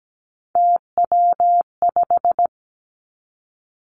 Morse code, heard as a keyed tone, about 17 wpm, 700 Hz. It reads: TW5
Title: Three Challenging Letter-Number Combinations 17wpm